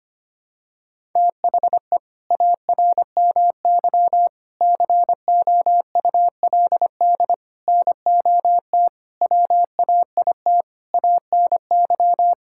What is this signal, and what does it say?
Morse code, heard as a keyed tone, about 25 wpm, 700 Hz. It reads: THE ARMY COULD NOT WAIT ANY
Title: Sentences from Top 1000 Words - Review Part 5 - 25wpm